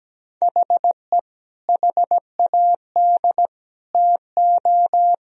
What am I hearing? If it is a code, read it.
Morse code, heard as a keyed tone, about 17 wpm, 700 Hz. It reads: HE HAD TO
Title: Sentences from Top 1000 Words - Review Part 2 - 17wpm